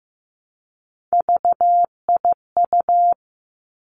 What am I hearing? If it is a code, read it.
Morse code, heard as a keyed tone, about 15 wpm, 700 Hz. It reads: VIU
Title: Three Challenging Letter-Number Combinations 15wpm